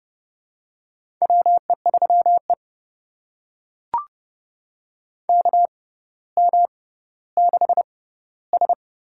Morse code, heard as a keyed tone, about 32 wpm, spaced as for 16 wpm, 700 Hz. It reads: WE3E KM6S
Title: USA Call Signs - 6x Character Spacing - 30wpm